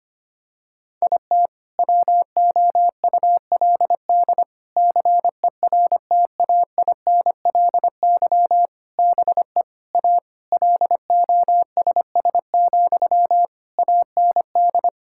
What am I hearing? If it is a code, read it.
Morse code, heard as a keyed tone, about 25 wpm, 700 Hz. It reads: IT WOULD CERTAINLY BE A LOSS, AND
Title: Sentences from Top 1600 Words 25wpm